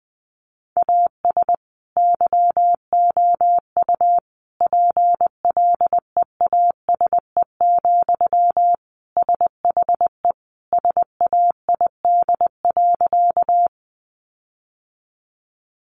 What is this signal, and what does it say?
Morse code, heard as a keyed tone, about 20 wpm, 700 Hz. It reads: AS YOU PLEASE, SHE SAID.
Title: Sentences from Top 1500 Words 20wpm